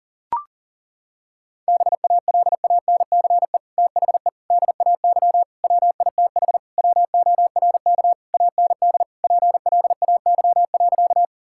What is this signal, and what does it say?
Morse code, heard as a keyed tone, about 40 wpm, 700 Hz. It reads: BALANCE THE DAY WITH WORK AND PLAY.
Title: Sentences from Top 1000 Words - Review Part 6 - 40wpm